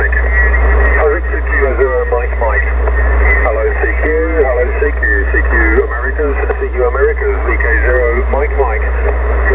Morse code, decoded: none